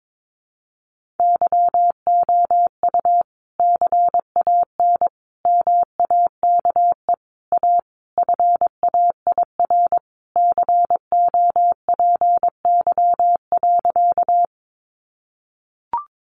Morse code, heard as a keyed tone, about 22 wpm, 700 Hz. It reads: YOU CAN MAKE A FAIR COPY.
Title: Sentences from Top 1400 Words 22wpm